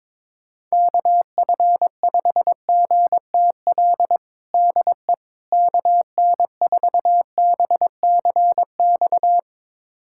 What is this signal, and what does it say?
Morse code, heard as a keyed tone, about 21 wpm, 700 Hz. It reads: KF5GTL DE KN4BCX
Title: USA Callsign DE Callsign 22wpm